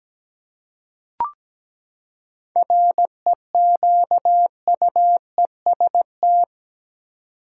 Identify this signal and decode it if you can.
Morse code, heard as a keyed tone, about 17 wpm, 700 Hz. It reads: REQUEST
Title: Seven Letter Words - 17wpm